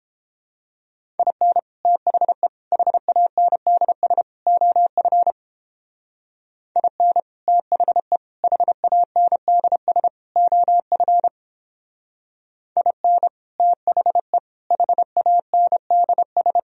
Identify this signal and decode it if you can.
Morse code, heard as a keyed tone, about 29 wpm, 700 Hz. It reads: IN THE HANDS OF IN THE HANDS OF IN THE HANDS